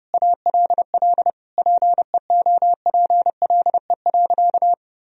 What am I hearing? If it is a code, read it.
Morse code, heard as a keyed tone, about 30 wpm, 700 Hz. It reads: ALL PEOPLE.
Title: Sentences from Top 1000 Words - Review Part 3 - 30wpm